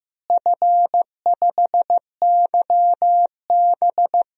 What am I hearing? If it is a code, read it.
Morse code, heard as a keyed tone, about 15 wpm, 700 Hz. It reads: F5YB
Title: DXCC-100 Call Signs - 4 Characters - Repeated 3x - 15wpm